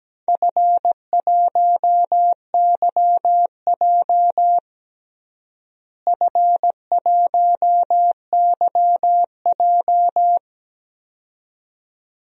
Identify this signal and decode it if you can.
Morse code, heard as a keyed tone, about 17 wpm, 700 Hz. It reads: F1YJ F1YJ
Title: DXCC-100 Call Signs - 4 Characters - Repeated 2x - 17wpm